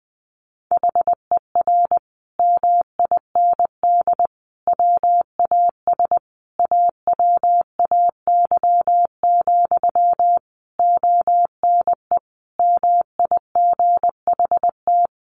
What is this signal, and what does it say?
Morse code, heard as a keyed tone, about 20 wpm, 700 Hz. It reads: HER MIND WAS AWAY, ONE MIGHT
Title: Sentences from Top 1500 Words 20wpm